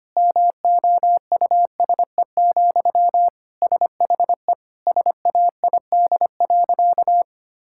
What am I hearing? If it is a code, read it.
Morse code, heard as a keyed tone, about 25 wpm, 700 Hz. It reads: MOUSE, SHE SAID.
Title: Sentences from Top 1000 Words - Review Part 5 - 25wpm